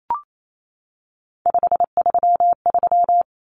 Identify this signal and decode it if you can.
Morse code, heard as a keyed tone, about 28 wpm, 700 Hz. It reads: 533
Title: Numbers - 3-Digit 28wpm